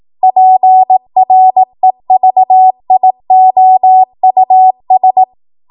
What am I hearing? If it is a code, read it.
Morse code, heard as a keyed tone, about 18 wpm, 750 Hz. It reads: PREVIOUS
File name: Archive / 18 WPM / 18WPM